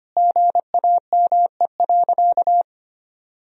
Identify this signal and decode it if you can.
Morse code, heard as a keyed tone, about 25 wpm, 700 Hz. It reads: GAME.
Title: Sentences from Top 1000 Words - Review Part 4 - 25wpm